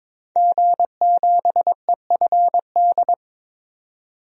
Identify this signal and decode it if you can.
Morse code, heard as a keyed tone, about 22 wpm, 700 Hz. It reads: G7EFD
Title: DXCC-100 Call Signs - 5 Characters - 22wpm